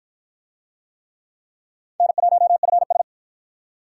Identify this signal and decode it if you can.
Morse code, heard as a keyed tone, about 51 wpm, 700 Hz. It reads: N1FS